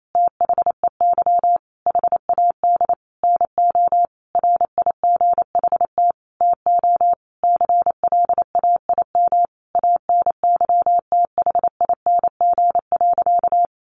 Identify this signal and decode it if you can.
Morse code, heard as a keyed tone, about 28 wpm, 700 Hz. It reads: THEY HAD NO RIGHT TO CLAIM ANYTHING.
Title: Sentences from Top 1500 Words 28wpm